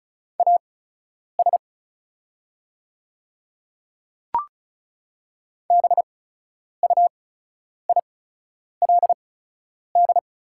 Morse code, heard as a keyed tone, about 36 wpm, spaced as for 10 wpm, 700 Hz. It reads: AS BUILD
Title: 8x Character Spacing - Top 300 Words - 35wpm